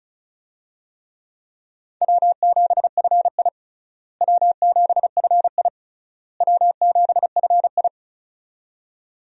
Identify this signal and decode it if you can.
Morse code, heard as a keyed tone, about 36 wpm, 700 Hz. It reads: W7FI W7FI W7FI
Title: USA Call Signs - 4 Characters - Repeated 3x - 35wpm